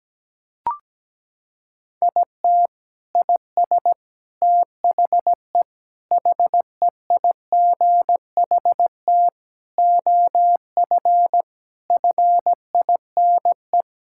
Morse code, heard as a keyed tone, about 17 wpm, 700 Hz. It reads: IT IS THE HEIGHT OF FINE
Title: Sentences from Top 1600 Words 17wpm